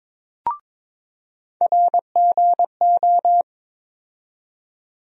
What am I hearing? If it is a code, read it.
Morse code, heard as a keyed tone, about 23 wpm, 700 Hz. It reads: RGO
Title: USA Call Sign Postfixes - 3 Letters 22wpm